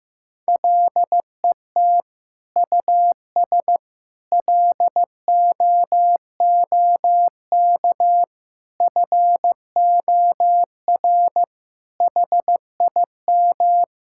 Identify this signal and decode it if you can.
Morse code, heard as a keyed tone, about 16 wpm, 700 Hz. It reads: LET US LOOK FOR HIM
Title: Sentences from Top 1600 Words 15wpm